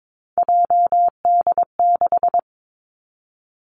Morse code, heard as a keyed tone, about 22 wpm, 700 Hz. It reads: JD6
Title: Three Challenging Letter-Number Combinations 22wpm